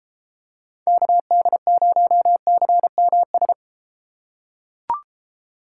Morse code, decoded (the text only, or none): KD0CMS